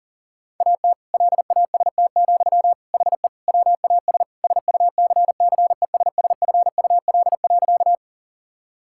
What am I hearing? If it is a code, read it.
Morse code, heard as a keyed tone, about 40 wpm, 700 Hz. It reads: AT LAST, HE WAS SUCCESSFUL.
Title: Sentences from Top 1000 Words - Review Part 6 - 40wpm